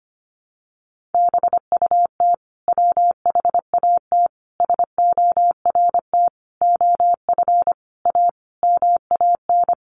Morse code, heard as a keyed tone, about 25 wpm, 700 Hz. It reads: BUT WHAT SORT OF A MAN
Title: Sentences from Top 1300 Words 25wpm